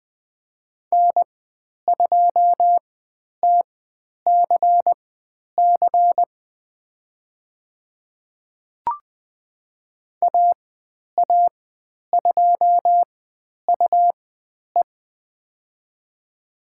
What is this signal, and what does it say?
Morse code, heard as a keyed tone, about 20 wpm, spaced as for 10 wpm, 700 Hz. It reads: N2TCC AA2UE
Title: 102 Lesson 17g - Letter u - Call Signs 20wpm